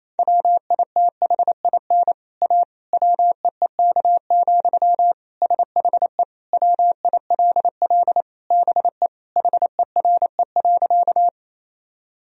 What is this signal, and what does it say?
Morse code, heard as a keyed tone, about 28 wpm, 700 Hz. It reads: WITHIN A WEEK, SHE WILL BE HERE.
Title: Sentences from Top 1000 Words - Review Part 4 - 28wpm